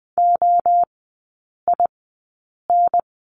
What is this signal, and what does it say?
Morse code, heard as a keyed tone, about 20 wpm, spaced as for 8 wpm, 700 Hz. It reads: OIN
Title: 053 Lesson 10e - Letter r - Three Characters 20wpm